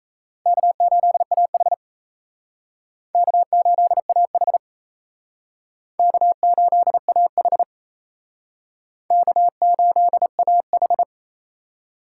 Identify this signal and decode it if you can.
Morse code, heard as a keyed tone, about 37 wpm, 700 Hz. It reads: K8AH K8AH K8AH K8AH